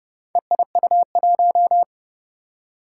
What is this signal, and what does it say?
Morse code, heard as a keyed tone, about 30 wpm, 700 Hz. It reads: EIU1
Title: Four Challenging Letter-Number Combinations 30wpm